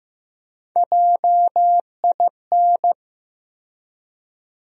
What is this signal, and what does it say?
Morse code, heard as a keyed tone, about 15 wpm, 700 Hz. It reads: JIN
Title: USA Call Sign Postfixes - 3 Letters 15wpm